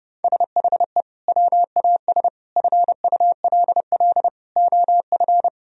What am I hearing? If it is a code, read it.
Morse code, heard as a keyed tone, about 30 wpm, 700 Hz. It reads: SHE WAS FULL OF
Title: Sentences from Top 1000 Words - Review Part 3 - 30wpm